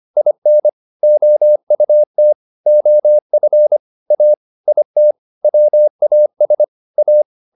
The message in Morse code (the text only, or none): IN OUT OF A IT WAS A